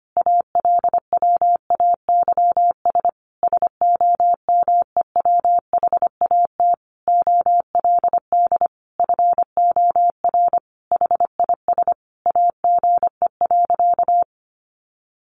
Morse code, heard as a keyed tone, about 25 wpm, 700 Hz. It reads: ALWAYS SOMEWHAT OLD FOR HIS AGE.
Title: Sentences from Top 1300 Words 25wpm